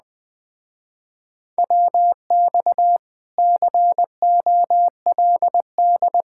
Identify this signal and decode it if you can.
Morse code, heard as a keyed tone, about 20 wpm, 700 Hz. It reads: WX COLD
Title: Speed-Racing Common QSO Phrases - 17wpm